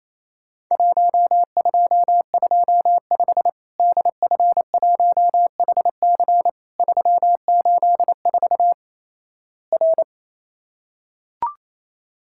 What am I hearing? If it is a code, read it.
Morse code, heard as a keyed tone, about 28 wpm, 700 Hz. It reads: 1225 DF1HC 384 R
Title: Worked All Europe DX Contest - QTC Exchange - 28wpm